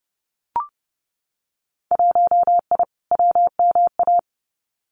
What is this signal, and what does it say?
Morse code, heard as a keyed tone, about 30 wpm, 700 Hz. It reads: 1I WMA
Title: Winter Field Day Exchanges Encore 30wpm